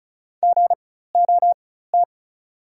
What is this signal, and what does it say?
Morse code, heard as a keyed tone, about 35 wpm, spaced as for 16 wpm, 690 Hz. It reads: GOT
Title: 4x Character Spacing - Top 500 Words - 35wpm